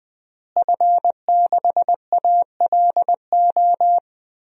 Morse code, heard as a keyed tone, about 20 wpm, 700 Hz. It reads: F6ALO
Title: DXCC-100 Call Signs - 5 Characters - Repeated 3x - 20wpm